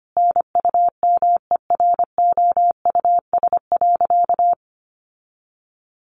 Morse code, heard as a keyed tone, about 25 wpm, 700 Hz. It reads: NUMEROUS.
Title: Sentences from Top 1400 Words 25wpm